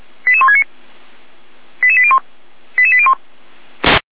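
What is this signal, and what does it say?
Five-tone selective call (ZVEI), heard as quick runs of five tones, tones of about 70 ms. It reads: 88169 88801 88801
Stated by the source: ZVEI five-tone selective call received on an SDR.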